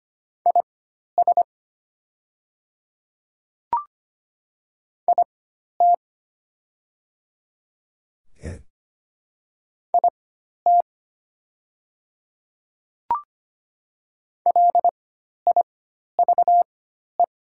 Morse code, heard as a keyed tone, about 25 wpm, spaced as for 12 wpm, 700 Hz. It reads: IS IT IT LIVE
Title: 4x Character Spacing - Top 200 Words - 25wpm